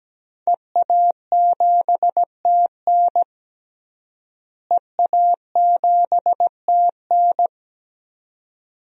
Morse code, heard as a keyed tone, about 17 wpm, 700 Hz. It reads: EA7TN EA7TN
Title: DXCC-100 Call Signs - 5 Characters - Repeated 2x - 17wpm